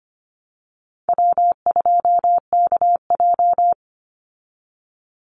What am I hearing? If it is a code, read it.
Morse code, heard as a keyed tone, about 25 wpm, 700 Hz. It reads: W2KJ
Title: Easy Call Signs - 25wpm